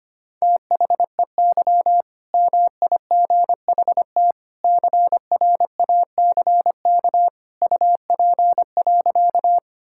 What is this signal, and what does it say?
Morse code, heard as a keyed tone, about 25 wpm, 700 Hz. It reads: THEY MIGHT CRACK UP.
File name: practice / 25wpm / All-Age Appropriate Jokes - 25wpm